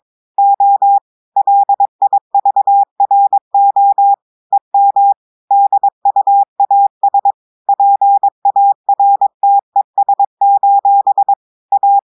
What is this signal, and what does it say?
Morse code, heard as a keyed tone, about 22 wpm, 800 Hz. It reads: O LIVRO EM DUAS PARTES: A
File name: audios / portugais / portugais texte en clair 1 vitesse 22 mots minutes